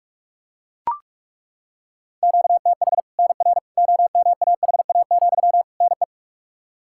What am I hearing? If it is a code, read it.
Morse code, heard as a keyed tone, about 43 wpm, 700 Hz. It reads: QTH NR OMAHA, NE
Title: Practice QSO - Exchange 1 - Beginning - 45wpm